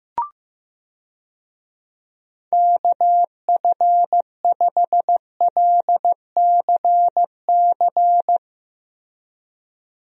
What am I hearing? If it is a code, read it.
Morse code, heard as a keyed tone, about 15 wpm, 700 Hz. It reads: KF5LCC
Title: USA Call Signs - 6 Characters - Repeated 2x - 15wpm